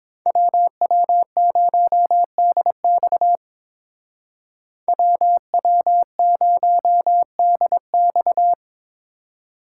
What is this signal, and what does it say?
Morse code, heard as a keyed tone, about 29 wpm, 700 Hz. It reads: WW0DX WW0DX